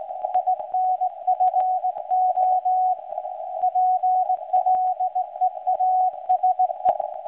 Morse code, heard as a keyed tone, about 18 wpm, 740 Hz. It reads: INFO PLEASE
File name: WebSDR records / SAQ recording start 17.2kHz